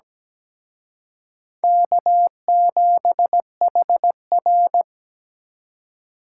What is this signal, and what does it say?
Morse code, heard as a keyed tone, about 20 wpm, 700 Hz. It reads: K7HR
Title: Speed Racing - USA 1x2 Call Signs 17wpm